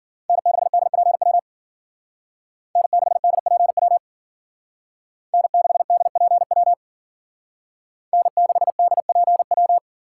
Morse code, heard as a keyed tone, about 50 wpm, 700 Hz. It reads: N6DPW N6DPW N6DPW N6DPW
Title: Speed Racing - USA 1x3 Call Signs 40wpm